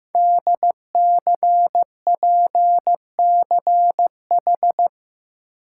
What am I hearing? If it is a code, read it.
Morse code, heard as a keyed tone, about 15 wpm, 700 Hz. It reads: DCPCH
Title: Code Groups - 5 Letters - 15wpm